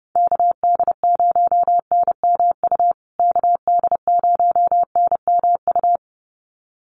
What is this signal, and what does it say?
Morse code, heard as a keyed tone, about 30 wpm, 700 Hz. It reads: KD0NMU KD0NMU
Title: USA Call Signs Repeated 2x 30wpm